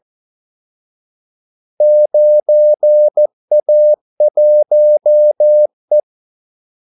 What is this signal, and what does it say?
Morse code, heard as a keyed tone, about 14 wpm, 600 Hz. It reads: 9A1E
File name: Callsigns-14wpm